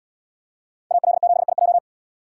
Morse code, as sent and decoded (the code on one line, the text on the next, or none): .- .-.. --... .. .---
AL7IJ